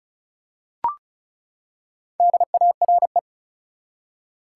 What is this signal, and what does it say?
Morse code, heard as a keyed tone, about 34 wpm, 700 Hz. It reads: DARE
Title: Four Letter Words - 35wpm